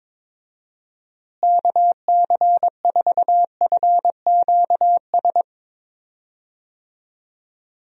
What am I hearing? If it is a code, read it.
Morse code, heard as a keyed tone, about 22 wpm, 700 Hz. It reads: KC4FQS
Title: USA Call Signs - 6 Characters - Repeated 3x - 22wpm